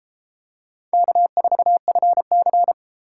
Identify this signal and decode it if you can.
Morse code, heard as a keyed tone, about 33 wpm, 700 Hz. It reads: K4FC